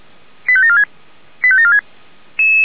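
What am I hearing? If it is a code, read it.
Five-tone selective call (ZVEI), heard as quick runs of five tones, tones of about 70 ms. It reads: 85746 85746
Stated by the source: ZVEI five-tone call 85827 received by SDR.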